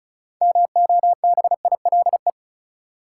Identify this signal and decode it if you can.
Morse code, heard as a keyed tone, about 36 wpm, 700 Hz. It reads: MOBILE
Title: Six Letter Words - 35wpm